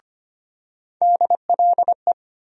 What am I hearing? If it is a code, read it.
Morse code, heard as a keyed tone, about 25 wpm, 700 Hz. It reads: DLE
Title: USA Call Sign Postfixes - 3 Letters 25wpm